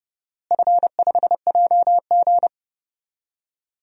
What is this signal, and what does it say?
Morse code, heard as a keyed tone, about 30 wpm, 700 Hz. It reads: F5JG
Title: DXCC-100 Call Signs - 4 Characters - Repeated 2x - 30wpm